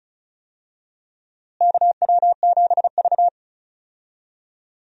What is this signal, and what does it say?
Morse code, heard as a keyed tone, about 36 wpm, 700 Hz. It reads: KW7V
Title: USA Call Signs - 4 Characters - Repeated 2x - 35wpm